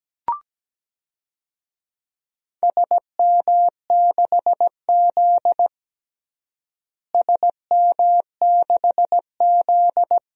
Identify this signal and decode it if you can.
Morse code, heard as a keyed tone, about 17 wpm, 700 Hz. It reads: SM6Z SM6Z
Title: DXCC-100 Call Signs - 4 Characters - Repeated 2x - 17wpm